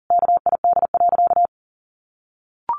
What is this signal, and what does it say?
Morse code, heard as a keyed tone, about 40 wpm, 700 Hz. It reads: KID.
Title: A Sound Of Thunder 40wpm